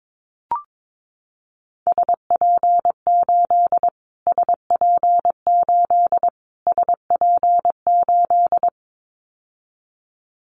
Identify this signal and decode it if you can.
Morse code, heard as a keyed tone, about 22 wpm, 700 Hz. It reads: SP8 SP8 SP8
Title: Callsign Prefixes for Top 25 DXCC - Repeated 3x - 22wpm